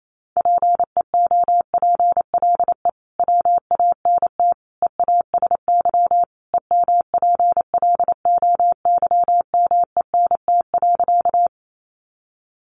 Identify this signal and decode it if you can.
Morse code, heard as a keyed tone, about 28 wpm, 700 Hz. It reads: PEOPLE WANT EASY EMPLOYMENT.
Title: Sentences from Top 1500 Words 28wpm